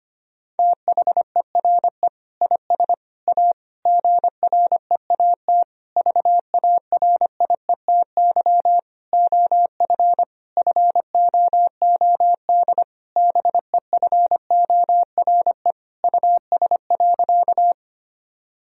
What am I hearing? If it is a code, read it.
Morse code, heard as a keyed tone, about 25 wpm, 700 Hz. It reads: THERE IS A GREAT VARIETY OF FOOD BEFORE US.
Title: Sentences from Top 1000 Words - Review Part 4 - 25wpm